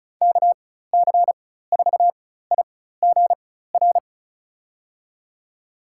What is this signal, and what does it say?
Morse code, heard as a keyed tone, about 35 wpm, spaced as for 16 wpm, 700 Hz. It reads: KC4IGR
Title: USA Call Signs - 4x Character Spacing - 35wpm